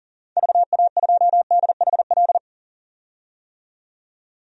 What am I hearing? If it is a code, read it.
Morse code, heard as a keyed tone, about 40 wpm, 700 Hz. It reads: VA2DHL
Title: DXCC-100 Call Signs - 6 Characters - 40wpm